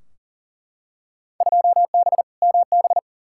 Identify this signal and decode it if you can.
Morse code, heard as a keyed tone, about 40 wpm, 700 Hz. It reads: 2B MB
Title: Field Day Exchanges Encore - 40wpm